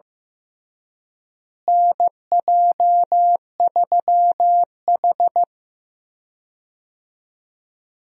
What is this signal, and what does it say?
Morse code, heard as a keyed tone, about 15 wpm, 700 Hz. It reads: NJ3H
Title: USA Call Signs - 4 Characters - Repeated 3x - 15wpm